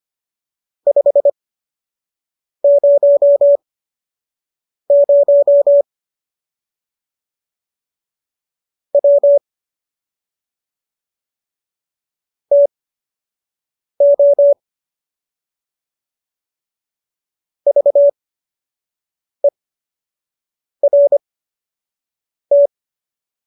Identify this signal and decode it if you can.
Morse code, heard as a keyed tone, about 25 wpm, spaced as for 6 wpm, 570 Hz. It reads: 500 W TO VERT